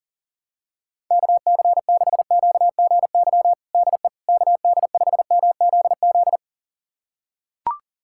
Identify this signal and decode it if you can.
Morse code, heard as a keyed tone, about 40 wpm, 700 Hz. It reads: KC6QGY DE KD5MZZ
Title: USA Callsign DE Callsign - 6 Characters - 40wpm